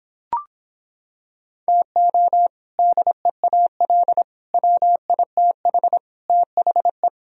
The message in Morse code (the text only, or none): TO DEAL WITH THE